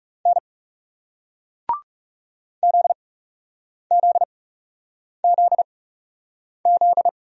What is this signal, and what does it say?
Morse code, heard as a keyed tone, about 42 wpm, spaced as for 8 wpm, 700 Hz. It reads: N ZZZZ